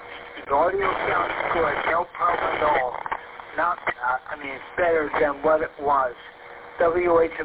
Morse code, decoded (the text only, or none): none